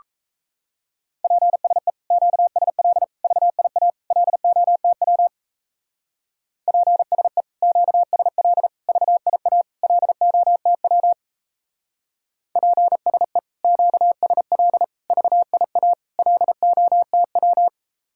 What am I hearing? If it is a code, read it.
Morse code, heard as a keyed tone, about 34 wpm, 700 Hz. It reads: PSE QSL VIA LOTW PSE QSL VIA LOTW PSE QSL VIA LOTW
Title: Speed-Racing Common QSO Phrases - 28wpm